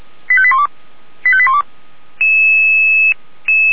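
Five-tone selective call (ZVEI), heard as quick runs of five tones, tones of about 70 ms. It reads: 85812 85812
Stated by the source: Five-tone ZVEI selective call received by SDR, from the fire brigade Feuerwehr Orlamünde.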